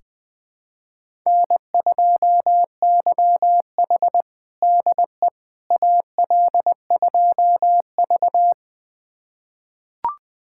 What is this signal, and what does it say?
Morse code, heard as a keyed tone, about 20 wpm, 700 Hz. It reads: N2YH DE AL2V
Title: USA Callsign DE Callsign - 4 Characters - 20wpm